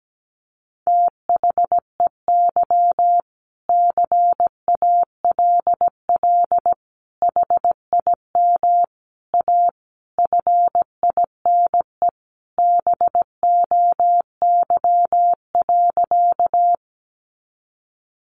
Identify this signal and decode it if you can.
Morse code, heard as a keyed tone, about 17 wpm, 700 Hz. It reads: THEY CALL HIM A FINE BOY.
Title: Sentences from Top 500 Words 17wpm